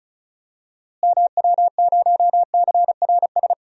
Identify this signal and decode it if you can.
Morse code, heard as a keyed tone, about 36 wpm, 700 Hz. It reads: MW0CRS
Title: DXCC-100 Call Signs - 6 Characters - Repeated 2x - 35wpm